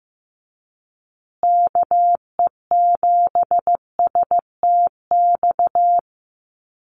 Morse code, heard as a keyed tone, about 15 wpm, 700 Hz. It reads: KE7STX
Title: USA Call Signs Repeated 2x 15wpm